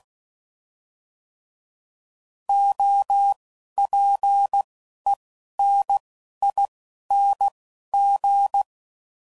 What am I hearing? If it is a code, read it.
Morse code, heard as a keyed tone, about 16 wpm, spaced as for 12 wpm, 800 Hz. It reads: OPENING
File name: Prefixes / By First Letter / Os